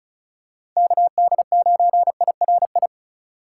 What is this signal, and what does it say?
Morse code, heard as a keyed tone, about 36 wpm, 700 Hz. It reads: KD9IRI
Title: USA Call Signs - 6 Characters - 3x Character Spacing - 35wpm